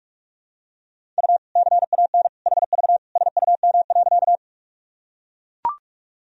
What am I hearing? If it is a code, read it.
Morse code, heard as a keyed tone, about 42 wpm, 700 Hz. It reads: U CAN HV SUM.
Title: Sentences from Top 100 Words and Abbreviations 45wpm